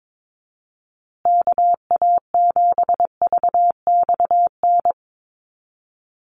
Morse code, heard as a keyed tone, about 22 wpm, 700 Hz. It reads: KA7VXN
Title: USA Call Signs Repeated 3x 22wpm